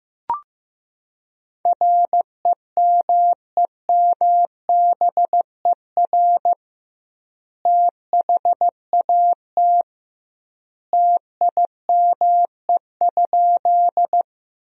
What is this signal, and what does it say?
Morse code, heard as a keyed tone, about 15 wpm, 700 Hz. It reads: REMEMBER THAT TIME?
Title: 2x Word-Spacing - Sentences from Top 400 Words - 15wpm